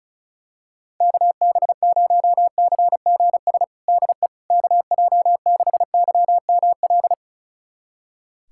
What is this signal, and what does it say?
Morse code, heard as a keyed tone, about 36 wpm, 700 Hz. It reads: KB0CGS DE KJ6YML
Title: USA Callsign DE Callsign - 6 Characters - 35wpm